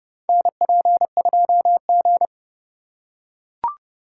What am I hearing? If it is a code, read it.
Morse code, heard as a keyed tone, about 30 wpm, 700 Hz. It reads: NP2G
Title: USA Call Signs - 4 Characters - 2x Character Spacing - 30wpm